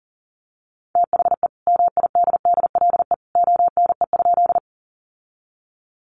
Morse code, heard as a keyed tone, about 40 wpm, 700 Hz. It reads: THE MIDDLE ONE?
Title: Sentences from Top 900 Words 40wpm